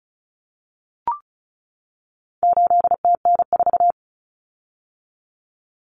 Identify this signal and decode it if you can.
Morse code, heard as a keyed tone, about 34 wpm, 700 Hz. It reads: 8TN4
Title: Common Cut Numbers - 4-Digit 35wpm